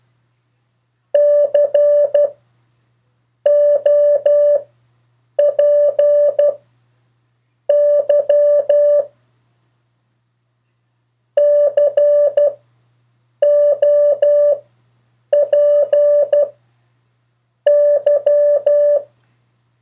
Morse code, heard as a keyed tone, about 12 wpm, spaced as for 6 wpm, 580 Hz. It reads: COPY COPY
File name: 3of4-words-2